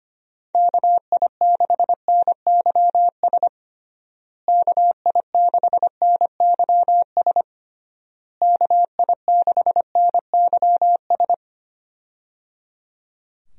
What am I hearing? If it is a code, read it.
Morse code, heard as a keyed tone, about 25 wpm, 700 Hz. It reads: KI6NYS KI6NYS KI6NYS
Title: USA Call Signs - 6 Characters - Repeated 3x - 25wpm